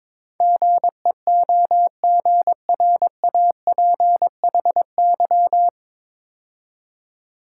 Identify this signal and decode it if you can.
Morse code, heard as a keyed tone, about 22 wpm, 700 Hz. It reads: GEOGRAPHY
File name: 9 Letter Words - First 3 Letters Given - 22wpm